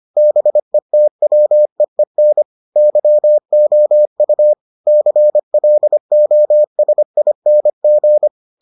Morse code, heard as a keyed tone, about 25 wpm, 600 Hz. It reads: BETWEEN YOU CLOSING